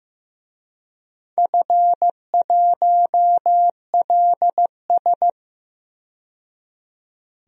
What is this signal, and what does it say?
Morse code, heard as a keyed tone, about 15 wpm, 700 Hz. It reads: F1LS
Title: DXCC-100 Call Signs - 4 Characters - 15wpm